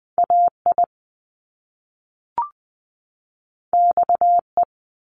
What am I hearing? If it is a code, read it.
Morse code, heard as a keyed tone, about 20 wpm, 700 Hz. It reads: AI XE
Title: Top Two Letter Combinations in Words 20wpm